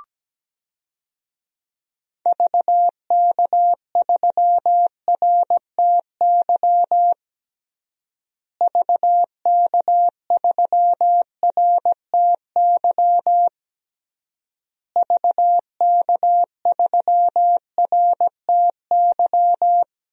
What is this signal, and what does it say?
Morse code, heard as a keyed tone, about 17 wpm, 700 Hz. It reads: VK3RTY VK3RTY VK3RTY
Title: DXCC-100 Call Signs - 6 Characters - Repeated 3x - 17wpm